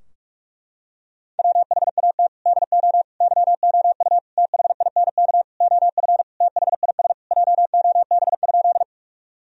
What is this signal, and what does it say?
Morse code, heard as a keyed tone, about 46 wpm, 700 Hz. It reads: WHAT DO YOU THINK OF THIS JOB?
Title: Sentences from Top 1000 Words - Review Part 3 - 45wpm